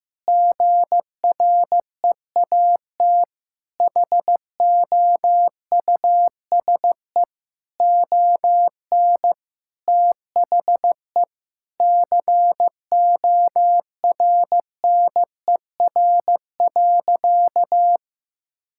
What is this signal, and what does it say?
Morse code, heard as a keyed tone, about 15 wpm, 700 Hz. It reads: GREAT HOUSE ON THE CORNER.
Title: Sentences from Top 1600 Words 15wpm